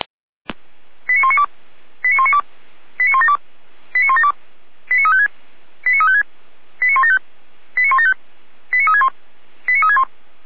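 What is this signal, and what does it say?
Five-tone selective call (ZVEI), heard as quick runs of five tones, tones of about 70 ms. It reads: 89192 89192 89172 89172 89356 89356 89175 89175 89361 89361
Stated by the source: ZVEI five-tone selective call received by SDR.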